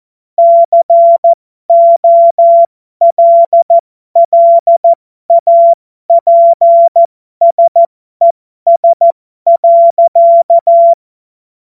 Morse code, heard as a keyed tone, about 14 wpm, spaced as for 12 wpm, 680 Hz. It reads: COLLAPSES.
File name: Morsekurs / Witze